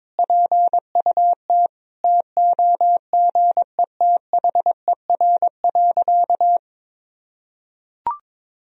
Morse code, heard as a keyed tone, about 22 wpm, 700 Hz. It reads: PUT TOGETHER.
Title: Sentences from Top 1000 Words - Review Part 3 - 22wpm